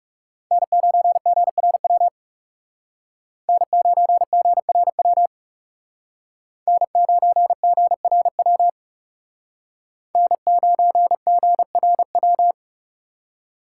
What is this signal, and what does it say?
Morse code, heard as a keyed tone, about 38 wpm, 700 Hz. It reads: N9GRW N9GRW N9GRW N9GRW